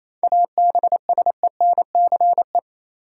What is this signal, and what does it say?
Morse code, heard as a keyed tone, about 28 wpm, 700 Hz. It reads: ABSENCE
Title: Sets of 4 Words - New Word Drill - 28wpm